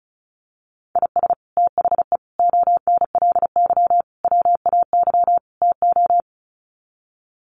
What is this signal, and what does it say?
Morse code, heard as a keyed tone, about 34 wpm, 700 Hz. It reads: IS THE ONLY WAY TO
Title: Sets of 5 Words Encore 35wpm